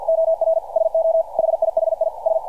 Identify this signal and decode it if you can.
Morse code, heard as a keyed tone, about 47 wpm, 660 Hz. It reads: OR SO HIHI S
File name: archived / CRSnet